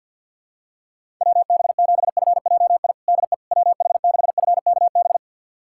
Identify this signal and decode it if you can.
Morse code, heard as a keyed tone, about 51 wpm, 700 Hz. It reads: WB7FJI DE WH6FKB
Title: USA Callsign DE Callsign - 6 Characters - 50wpm